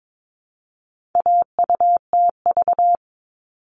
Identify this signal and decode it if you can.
Morse code, heard as a keyed tone, about 22 wpm, 700 Hz. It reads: AUTV
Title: All Cut Numbers - 4-Digit 22wpm